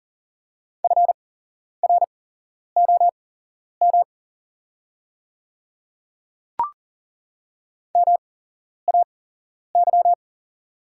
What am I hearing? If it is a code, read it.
Morse code, heard as a keyed tone, about 40 wpm, spaced as for 11 wpm, 690 Hz. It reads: FROM MAY
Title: 8x Character Spacing - Top 100 Words - 40wpm